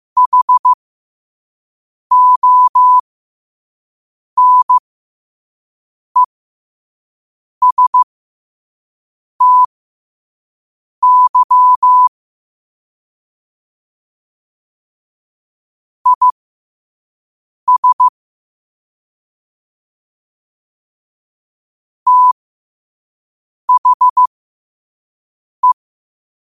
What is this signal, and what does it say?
Morse code, heard as a keyed tone, about 15 wpm, spaced as for 5 wpm, 1000 Hz. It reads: HONESTY IS THE
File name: qotdfiles / 5wpm